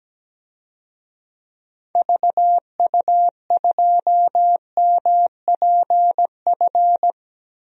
Morse code, heard as a keyed tone, about 17 wpm, 700 Hz. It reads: VU2MPF
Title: DXCC-100 Call Signs - 6 Characters - Repeated 2x - 17wpm